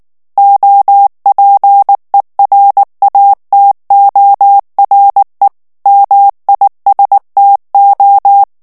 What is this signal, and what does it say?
Morse code, heard as a keyed tone, about 19 wpm, 800 Hz. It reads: OPERATORE MISTO
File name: morse mp3 / ITALIEN texte clair 2 vitesse 18 mots minute